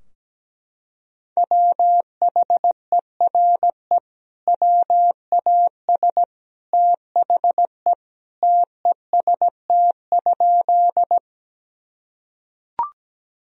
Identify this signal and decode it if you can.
Morse code, heard as a keyed tone, about 17 wpm, 700 Hz. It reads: WHERE WAS THE TEST?
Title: Sentences from Top 1000 Words - Review Part 3 - 17wpm